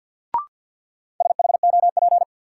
Morse code, heard as a keyed tone, about 52 wpm, 700 Hz. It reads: SHOP
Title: Sentences from Top 1600 Words - New Word Drill Alt - 50wpm